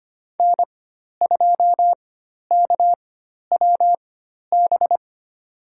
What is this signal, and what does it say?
Morse code, heard as a keyed tone, about 25 wpm, spaced as for 12 wpm, 700 Hz. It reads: N2KWB
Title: USA Call Signs - 5 Characters - 4x Character Spacing - 25wpm